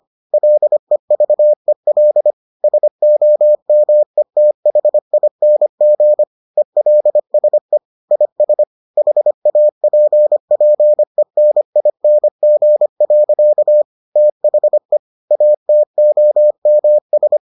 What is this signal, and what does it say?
Morse code, heard as a keyed tone, about 25 wpm, 600 Hz. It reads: LEVEL SOMETHING ELSE IS HAPPENING. THE ATOMS